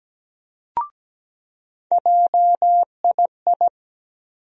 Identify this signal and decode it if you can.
Morse code, heard as a keyed tone, about 17 wpm, 700 Hz. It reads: JII
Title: USA Call Sign Postfixes - 3 Letters 17wpm